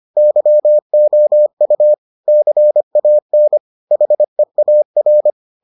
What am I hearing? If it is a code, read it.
Morse code, heard as a keyed tone, about 25 wpm, 600 Hz. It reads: YOU CAN HEAR